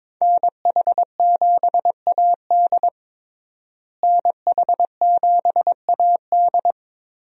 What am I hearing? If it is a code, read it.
Morse code, heard as a keyed tone, about 21 wpm, 700 Hz. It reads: NH7AD NH7AD
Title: USA Call Signs - 5 Characters - Repeated 3x - 22wpm